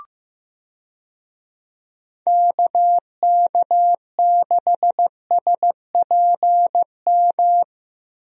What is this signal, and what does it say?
Morse code, heard as a keyed tone, about 16 wpm, 700 Hz. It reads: KK6SPM
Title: USA Call Signs - 6 Characters - Repeated 2x - 15wpm